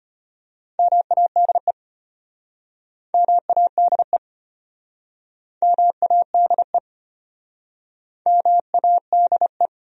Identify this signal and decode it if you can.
Morse code, heard as a keyed tone, about 33 wpm, 700 Hz. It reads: MADE MADE MADE MADE